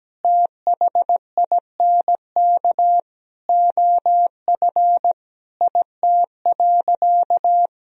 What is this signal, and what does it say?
Morse code, heard as a keyed tone, about 17 wpm, 700 Hz. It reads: THINK OF IT.
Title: Sentences from Top 200 Words and Abbreviations 17wpm